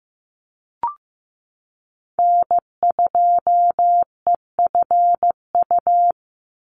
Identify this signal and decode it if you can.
Morse code, heard as a keyed tone, about 15 wpm, 700 Hz. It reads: N2EFU
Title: USA Call Signs 15wpm